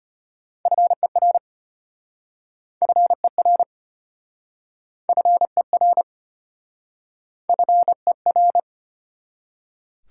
Morse code, heard as a keyed tone, about 33 wpm, 700 Hz. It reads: FER FER FER FER